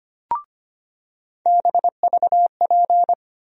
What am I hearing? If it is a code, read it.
Morse code, heard as a keyed tone, about 25 wpm, 700 Hz. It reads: BVP
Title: USA Call Sign Postfixes - 3 Letters 25wpm